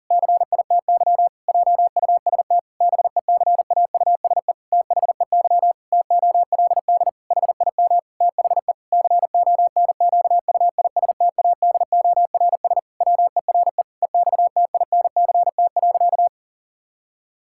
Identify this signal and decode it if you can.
Morse code, heard as a keyed tone, about 40 wpm, 700 Hz. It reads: CITY JUST BECAUSE THEY TOLD HIM THE CONQUISTADORS WERE EXTINCT.
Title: Pangram Challenge v2 - 40wpm